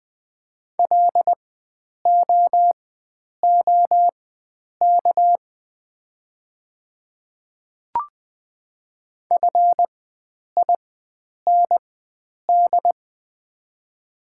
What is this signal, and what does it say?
Morse code, heard as a keyed tone, about 20 wpm, spaced as for 9 wpm, 700 Hz. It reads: LOOK FIND
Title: 4x Character Spacing - Top 100 Words - 20wpm